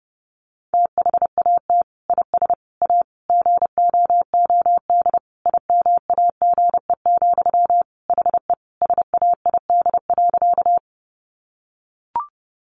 Morse code, heard as a keyed tone, about 30 wpm, 700 Hz. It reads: THAT IS A GOOD IMAGE, HE SAID.
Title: Sentences from Top 700 Words 30wpm